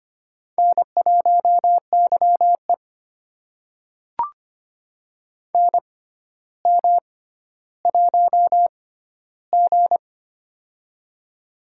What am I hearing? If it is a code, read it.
Morse code, heard as a keyed tone, about 25 wpm, spaced as for 13 wpm, 700 Hz. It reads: N1YE NM1G
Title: USA Call Signs - 4 Characters - 6x Character Spacing - 25wpm